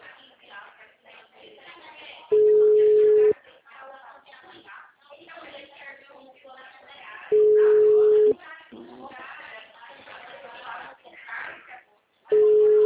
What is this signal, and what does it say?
Ringback tone (Europe, 425 Hz), with speech.